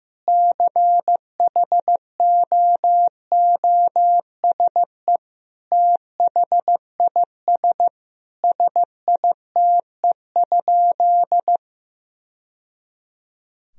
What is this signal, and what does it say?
Morse code, heard as a keyed tone, about 15 wpm, 700 Hz. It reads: CHOOSE THIS SITE?
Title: Sentences from Top 1000 Words - Review Part 4 - 15wpm